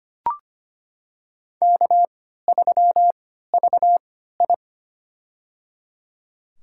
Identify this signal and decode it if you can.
Morse code, heard as a keyed tone, about 25 wpm, spaced as for 14 wpm, 700 Hz. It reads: K3VI
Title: USA Call Signs - 3x Character Spacing - 25wpm